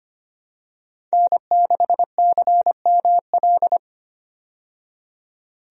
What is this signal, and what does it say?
Morse code, heard as a keyed tone, about 25 wpm, 700 Hz. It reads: N6CML
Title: USA Call Signs - 5 Characters - 25wpm